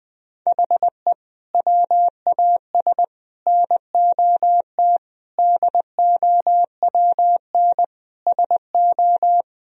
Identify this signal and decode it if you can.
Morse code, heard as a keyed tone, about 21 wpm, 700 Hz. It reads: HE WAS NOT DOWN SO
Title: Sentences from Top 1000 Words - Review Part 1 - 20wpm